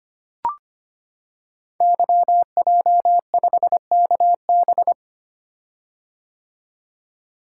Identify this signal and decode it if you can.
Morse code, heard as a keyed tone, about 25 wpm, 700 Hz. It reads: YJ5KB
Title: Code Groups - 5 Letter:Numbers - 25wpm